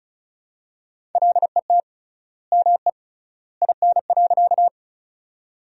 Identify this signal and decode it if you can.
Morse code, heard as a keyed tone, about 36 wpm, 700 Hz. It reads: LET ME IN.
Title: 3x Word-Spacing - Sentences from Top 300 Words - 35wpm